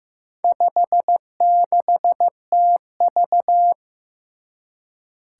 Morse code, heard as a keyed tone, about 15 wpm, 700 Hz. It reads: 56TV
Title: Four Challenging Letter-Number Combinations 15wpm